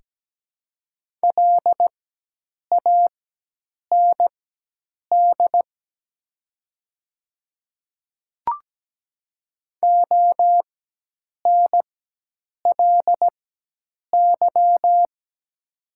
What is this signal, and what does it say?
Morse code, heard as a keyed tone, about 17 wpm, spaced as for 8 wpm, 700 Hz. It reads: LAND ONLY
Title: 4x Character Spacing - Top 300 Words - 17wpm